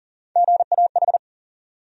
Morse code, heard as a keyed tone, about 40 wpm, 700 Hz. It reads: GAH
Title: USA Call Sign Postfixes - 3 Letters 40wpm